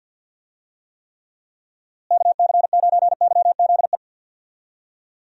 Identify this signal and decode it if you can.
Morse code, heard as a keyed tone, about 51 wpm, 700 Hz. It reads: KC9YZE
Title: USA Call Signs - 6 Characters - Repeated 3x - 50wpm